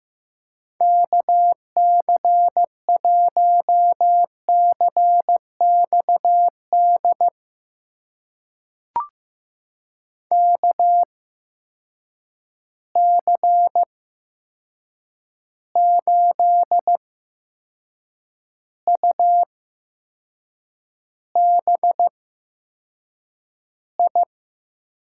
Morse code, heard as a keyed tone, about 15 wpm, spaced as for 6 wpm, 700 Hz. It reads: KC1CXD KC8UBI
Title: USA Call Signs - 6 Characters - 8x Character Spacing - 15wpm